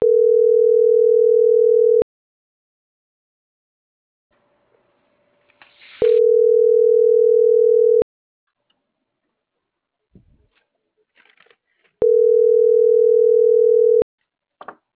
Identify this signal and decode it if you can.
Ringback tone (North America), 440 + 480 Hz.